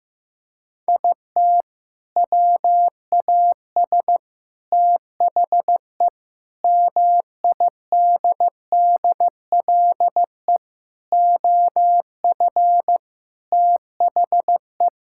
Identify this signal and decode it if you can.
Morse code, heard as a keyed tone, about 15 wpm, 700 Hz. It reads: IT WAS THE MIDDLE OF THE
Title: Sentences from Top 1000 Words - Review Part 6 - 15wpm